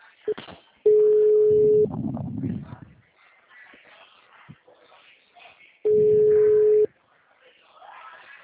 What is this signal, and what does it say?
Ringback tone (Europe, 425 Hz), with speech.